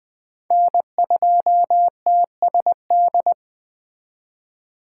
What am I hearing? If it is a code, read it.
Morse code, heard as a keyed tone, about 19 wpm, 700 Hz. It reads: N2TSD